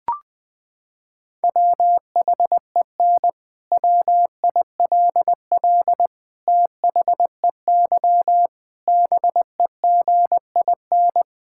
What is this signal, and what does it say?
Morse code, heard as a keyed tone, about 20 wpm, 700 Hz. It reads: WHEN WILL THEY BEGIN
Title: Sentences from Top 1000 Words - Review Part 6 - 20wpm